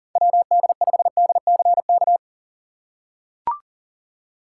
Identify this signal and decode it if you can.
Morse code, heard as a keyed tone, about 37 wpm, 700 Hz. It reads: WD5DCK
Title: USA Call Signs - 6x Character Spacing - 40wpm